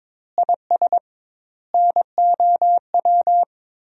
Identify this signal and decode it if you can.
Morse code, heard as a keyed tone, about 22 wpm, 700 Hz. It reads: IS NOW
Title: 2x Word-Spacing - Sentences from Top 400 Words - 22wpm